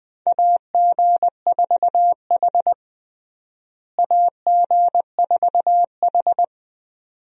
Morse code, heard as a keyed tone, about 20 wpm, 700 Hz. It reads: AG4H AG4H
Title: USA Call Signs - 4 Characters - Repeated 2x - 20wpm